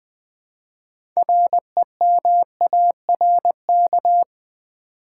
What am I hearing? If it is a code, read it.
Morse code, heard as a keyed tone, about 20 wpm, 700 Hz. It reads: REMARK
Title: Build A Word - 2 Letters At A Time - 6-Letter Words - 20wpm